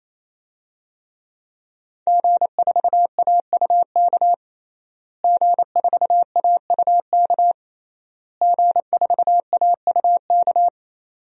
Morse code, heard as a keyed tone, about 28 wpm, 700 Hz. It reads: G4AUK G4AUK G4AUK
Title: DXCC-100 Call Signs - 5 Characters - Repeated 3x - 28wpm